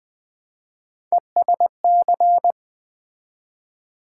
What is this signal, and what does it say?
Morse code, heard as a keyed tone, about 20 wpm, 700 Hz. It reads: ESC
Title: Build A Word - 3 Letters At A Time - 8-Letter Words - 20wpm